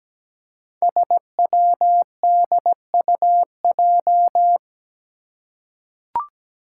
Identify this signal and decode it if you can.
Morse code, heard as a keyed tone, about 17 wpm, 700 Hz. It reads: SWDUJ
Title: Five Challenging Letter-Number Combinations 17wpm